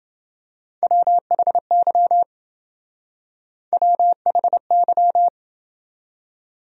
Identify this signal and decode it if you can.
Morse code, heard as a keyed tone, about 28 wpm, 700 Hz. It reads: WHY WHY